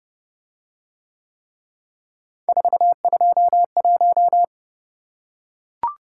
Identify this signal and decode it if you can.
Morse code, heard as a keyed tone, about 29 wpm, 700 Hz. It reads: 421